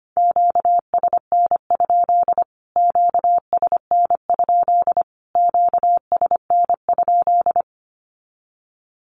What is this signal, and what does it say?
Morse code, heard as a keyed tone, about 25 wpm, 700 Hz. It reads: QSN? QSN? QSN?
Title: Q-Codes Repeated 3x 25wpm